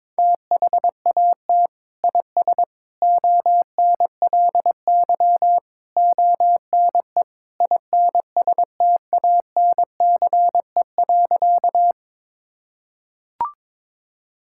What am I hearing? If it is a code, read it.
Morse code, heard as a keyed tone, about 22 wpm, 700 Hz. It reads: THAT IS ONLY ONE INSTANCE.
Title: Sentences from Top 1000 Words - Review Part 5 - 22wpm